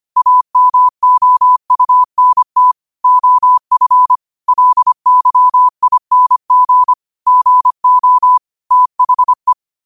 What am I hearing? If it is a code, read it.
Morse code, heard as a keyed tone, about 25 wpm, 1000 Hz. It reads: AMOUNT OF LYING GO THE
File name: qotdfiles / 25wpm